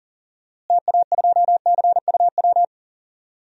Morse code, heard as a keyed tone, about 40 wpm, 700 Hz. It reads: TA2CUW
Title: DXCC-100 Call Signs - 6 Characters - Repeated 2x - 40wpm